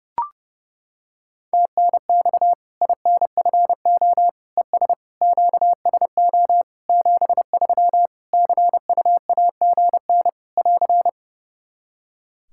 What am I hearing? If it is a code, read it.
Morse code, heard as a keyed tone, about 28 wpm, 700 Hz. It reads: TNX INFO ES QSO 73 CUAGN <AR>
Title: Practice QSO - Exchange 4 - Ending - 30wpm